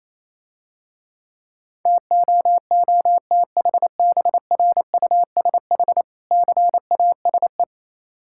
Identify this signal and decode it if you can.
Morse code, heard as a keyed tone, about 28 wpm, 700 Hz. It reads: TOOTHBRUSH CASE
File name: Easy Words - Speed-Racing with Hints - 22wpm